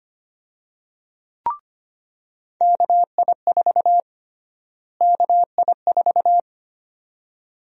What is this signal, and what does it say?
Morse code, heard as a keyed tone, about 25 wpm, 700 Hz. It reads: KI4 KI4
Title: USA Call Sign Prefixes - Repeated 2x - 25wpm